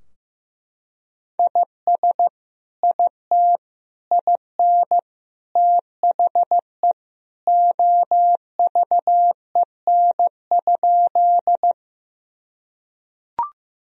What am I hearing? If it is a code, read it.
Morse code, heard as a keyed tone, about 15 wpm, 700 Hz. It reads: IS IT IN THE OVEN?
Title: Sentences from Top 1000 Words - Review Part 4 - 15wpm